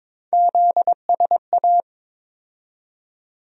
Morse code, heard as a keyed tone, about 22 wpm, 700 Hz. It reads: ZSA
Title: USA Call Sign Postfixes - 3 Letters 22wpm